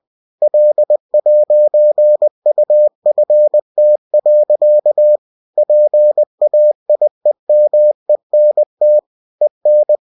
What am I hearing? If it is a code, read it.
Morse code, heard as a keyed tone, about 20 wpm, 600 Hz. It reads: L'UFT. PAIEMENT EN